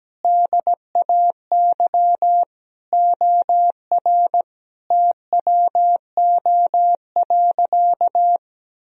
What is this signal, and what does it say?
Morse code, heard as a keyed tone, about 17 wpm, 700 Hz. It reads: DAY OR TWO.
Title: Sentences from Top 1000 Words - Review Part 1 - 17wpm